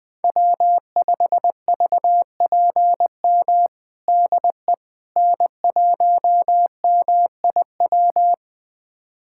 Morse code, heard as a keyed tone, about 20 wpm, 700 Hz. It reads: W5VPM DE N1MIW
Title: USA Callsign DE Callsign - 5 Characters - 20wpm